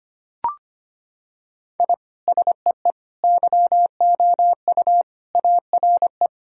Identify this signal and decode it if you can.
Morse code, heard as a keyed tone, about 25 wpm, 700 Hz. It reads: I SEE YOU ARE
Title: Sentences from Top 1000 Words - Review Part 3 - 25wpm